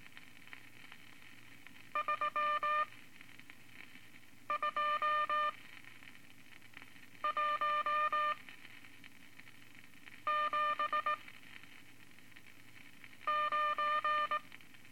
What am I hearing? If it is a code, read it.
Morse code, heard as a keyed tone, about 20 wpm, spaced as for 4 wpm, 1200 Hz. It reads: 32179
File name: Morse Code Records / 14B LCM 157 Units V Groups of Five